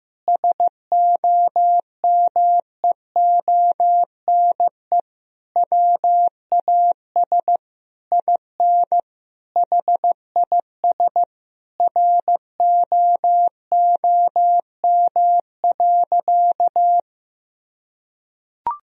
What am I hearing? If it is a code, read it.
Morse code, heard as a keyed tone, about 16 wpm, 700 Hz. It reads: SOMEONE WAS IN HIS ROOM.
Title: Sentences from Top 1000 Words - Review Part 4 - 15wpm